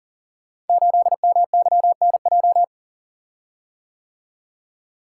Morse code, heard as a keyed tone, about 38 wpm, 690 Hz. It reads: 8MYNJ
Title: Code Groups - 5 Letter:Numbers - 40wpm